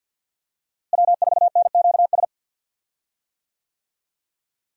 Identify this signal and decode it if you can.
Morse code, heard as a keyed tone, about 51 wpm, 700 Hz. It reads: W4NQS
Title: USA Call Signs - 5 Characters - Repeated 2x - 50wpm